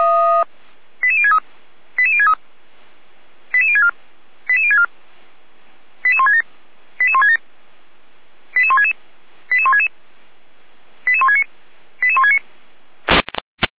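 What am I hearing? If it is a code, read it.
Five-tone selective call (ZVEI), heard as quick runs of five tones, tones of about 70 ms. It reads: 88063 88063 88064 88064 88167 88167 88166 88166 88169 88169
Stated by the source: ZVEI five-tone selective call received by SDR.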